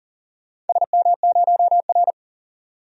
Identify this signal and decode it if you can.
Morse code, heard as a keyed tone, about 39 wpm, 700 Hz. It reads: SM0R
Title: DXCC-100 Call Signs - 4 Characters - 40wpm